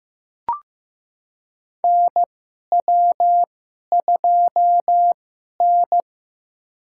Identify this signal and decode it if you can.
Morse code, heard as a keyed tone, about 15 wpm, spaced as for 11 wpm, 700 Hz. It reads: NW2N
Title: USA Call Signs - 4 Characters - 2x Character Spacing - 15wpm